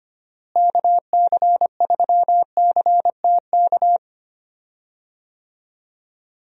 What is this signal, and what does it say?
Morse code, heard as a keyed tone, about 25 wpm, 700 Hz. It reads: KC3CTK
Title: USA Call Signs - 6 Characters - Repeated 3x - 25wpm